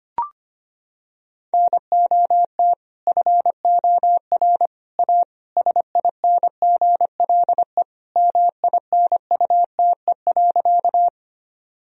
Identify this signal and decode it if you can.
Morse code, heard as a keyed tone, about 25 wpm, 700 Hz. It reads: NOT FOR A SINGLE MINUTE.
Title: Sentences from Top 1000 Words - Review Part 4 - 25wpm